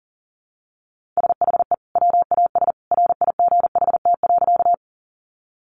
Morse code, heard as a keyed tone, about 40 wpm, 700 Hz. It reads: SHE WAS RIGHT.
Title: Sentences from Top 200 Words 40wpm